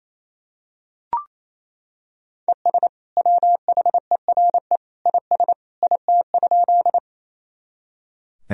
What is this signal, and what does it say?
Morse code, heard as a keyed tone, about 28 wpm, 700 Hz. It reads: ES WHERE IS IT?
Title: Sentences from Top 200 Words and Abbreviations 28wpm